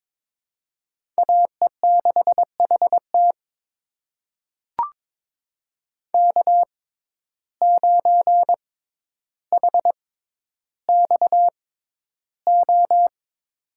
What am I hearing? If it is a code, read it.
Morse code, heard as a keyed tone, about 22 wpm, spaced as for 11 wpm, 700 Hz. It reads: AE6HT K9HXO
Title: USA Call Signs - 6x Character Spacing - 22wpm